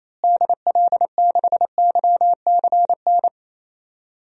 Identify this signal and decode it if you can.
Morse code, heard as a keyed tone, about 27 wpm, 700 Hz. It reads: DL6YCN
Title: DXCC-100 Call Signs - 6 Characters - Repeated 3x - 28wpm